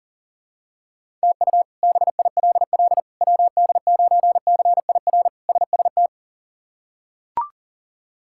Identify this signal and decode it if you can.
Morse code, heard as a keyed tone, about 40 wpm, 700 Hz. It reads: TU BILL WD9CIR SST
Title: K1USN SST QSO Practice - 40wpm